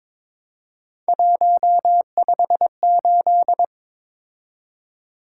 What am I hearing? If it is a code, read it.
Morse code, heard as a keyed tone, about 23 wpm, 700 Hz. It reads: 158
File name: practice / 22wpm / Three Letter-Number Combinations - 22wpm